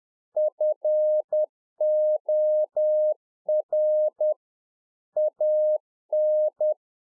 Morse code, heard as a keyed tone, about 10 wpm, 610 Hz. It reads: FOR AN